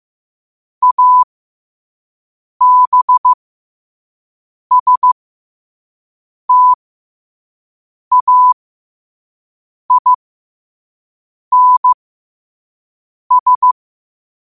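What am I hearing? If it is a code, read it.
Morse code, heard as a keyed tone, about 15 wpm, spaced as for 5 wpm, 1000 Hz. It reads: ABSTAINS
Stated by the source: Quote-of-the-day Morse code practice at 5 wpm.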